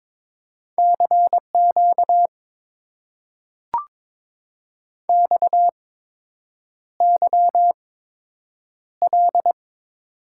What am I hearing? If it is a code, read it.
Morse code, heard as a keyed tone, about 22 wpm, spaced as for 8 wpm, 700 Hz. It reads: CQ XYL
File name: Top 90 QSO Elements - 8x Character Spacing - 22wpm